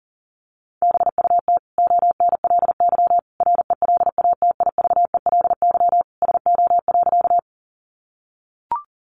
Morse code, heard as a keyed tone, about 40 wpm, 700 Hz. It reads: BUT ONLY RELATIVELY SO.
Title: Sentences from Top 1100 Words 40wpm